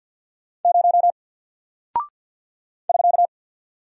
Morse code, heard as a keyed tone, about 50 wpm, 700 Hz. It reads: 0 <SK>
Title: Single Character (Letter:Number:Punctuation:Prosign) Alt 50wpm